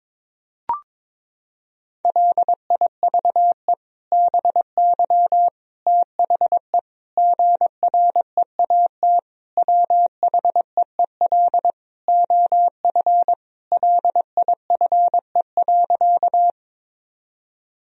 Morse code, heard as a keyed tone, about 21 wpm, 700 Hz. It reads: LIVE BY THE GREAT WHEEL OF LIFE.
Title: Sentences from Top 1000 Words - Review Part 3 - 22wpm